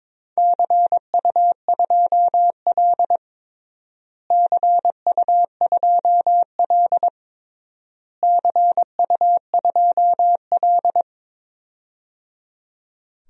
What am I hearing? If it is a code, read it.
Morse code, heard as a keyed tone, about 22 wpm, 700 Hz. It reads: CU2L CU2L CU2L
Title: DXCC-100 Call Signs - 4 Characters - Repeated 3x - 22wpm